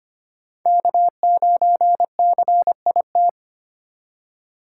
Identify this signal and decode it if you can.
Morse code, heard as a keyed tone, about 25 wpm, 700 Hz. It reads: K9CIT
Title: USA Call Signs - 5 Characters - 3x Character Spacing - 25wpm